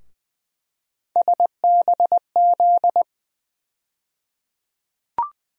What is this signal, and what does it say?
Morse code, heard as a keyed tone, about 21 wpm, 700 Hz. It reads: SBZ
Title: USA Call Sign Postfixes - 3 Letters 20wpm